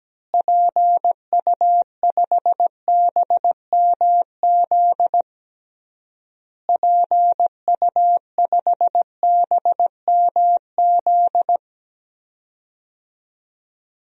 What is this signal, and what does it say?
Morse code, heard as a keyed tone, about 17 wpm, 700 Hz. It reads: PU5BMZ PU5BMZ
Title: DXCC-100 Call Signs - 6 Characters - Repeated 3x - 17wpm